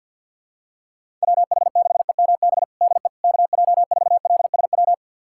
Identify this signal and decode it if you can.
Morse code, heard as a keyed tone, about 53 wpm, 700 Hz. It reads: WH6EMB DE KJ4LSW
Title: USA Callsign DE Callsign 50wpm